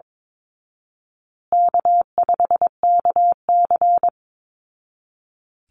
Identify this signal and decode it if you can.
Morse code, heard as a keyed tone, about 22 wpm, 700 Hz. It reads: K5KC
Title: Easy Call Signs Repeated 3x - 22wpm